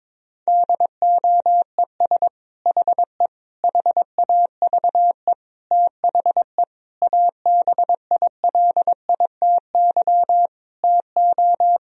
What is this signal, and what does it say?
Morse code, heard as a keyed tone, about 22 wpm, 700 Hz. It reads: DOES HE HAVE THE ABILITY TO
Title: Sentences from Top 1000 Words - Review Part 3 - 22wpm